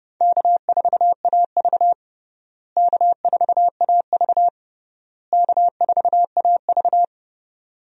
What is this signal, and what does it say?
Morse code, heard as a keyed tone, about 30 wpm, 700 Hz. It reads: K4AV K4AV K4AV
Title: USA Call Signs - 4 Characters - Repeated 3x - 30wpm